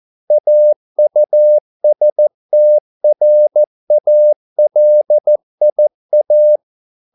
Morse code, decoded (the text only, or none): AUSTRALIA